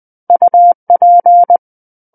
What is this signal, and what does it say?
Morse code, heard as a keyed tone, about 20 wpm, 700 Hz. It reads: UP
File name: 11-Chapter 11 - 20 WPM